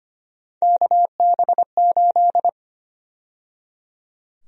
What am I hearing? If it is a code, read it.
Morse code, heard as a keyed tone, about 25 wpm, 700 Hz. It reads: KB8
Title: USA Call Sign Prefixes - 25wpm